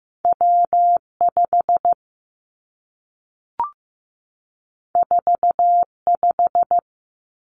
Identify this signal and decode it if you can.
Morse code, heard as a keyed tone, about 15 wpm, 700 Hz. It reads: W 5 4 5
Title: Two Challenging Letter-Number Combinations 15wpm